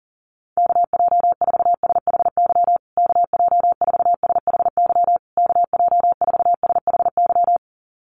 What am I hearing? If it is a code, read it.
Morse code, heard as a keyed tone, about 40 wpm, 700 Hz. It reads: KJ4SHY KJ4SHY KJ4SHY
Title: USA Call Signs Repeated 3x 40wpm